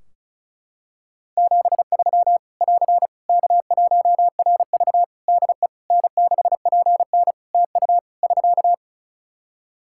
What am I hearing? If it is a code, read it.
Morse code, heard as a keyed tone, about 36 wpm, 700 Hz. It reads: 73 <AR> K1RV DE N6PN TU <SK>
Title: Practice QSO - Exchange 4 - Ending - 35wpm